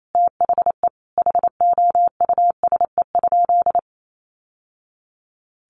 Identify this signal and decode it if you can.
Morse code, heard as a keyed tone, about 28 wpm, 700 Hz. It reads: THE HOUSE?
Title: Sentences from Top 1100 Words 28wpm